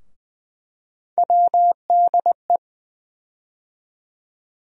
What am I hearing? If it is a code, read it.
Morse code, heard as a keyed tone, about 20 wpm, 700 Hz. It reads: WDE
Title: USA Call Sign Postfixes - 3 Letters 20wpm